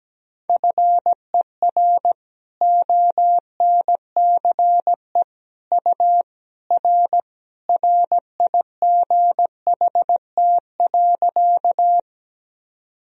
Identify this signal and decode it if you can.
Morse code, heard as a keyed tone, about 17 wpm, 700 Hz. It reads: FER ONCE U R RIGHT.
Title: Sentences from Top 300 Words and Abbreviations 17wpm